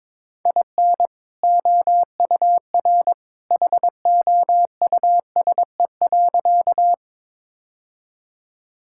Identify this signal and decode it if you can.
Morse code, heard as a keyed tone, about 22 wpm, 700 Hz. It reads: IN OUR HOUSE.
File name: practice / 22wpm / Sentences with 2-Word Phrases - 22wpm